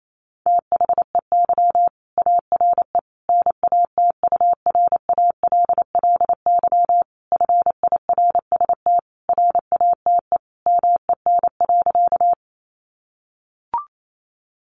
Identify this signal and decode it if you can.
Morse code, heard as a keyed tone, about 28 wpm, 700 Hz. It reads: THEY ARE NATURALLY FIRST RATE MEN.
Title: Sentences from Top 1300 Words 28wpm